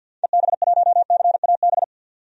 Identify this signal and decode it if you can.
Morse code, heard as a keyed tone, about 53 wpm, 700 Hz. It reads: EB1CAB
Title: DXCC-100 Call Signs - 6 Characters - 50wpm